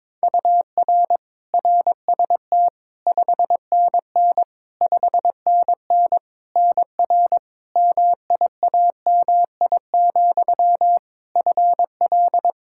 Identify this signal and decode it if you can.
Morse code, heard as a keyed tone, about 22 wpm, 700 Hz. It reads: UR RST 5NN 5NN NR MIAMI, FL
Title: K1USN SST QSO Practice 22wpm